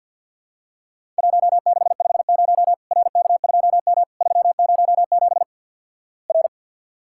Morse code, heard as a keyed tone, about 48 wpm, 700 Hz. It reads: 1650 RK2M 307 R
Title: Worked All Europe DX Contest - QTC Exchange - 50wpm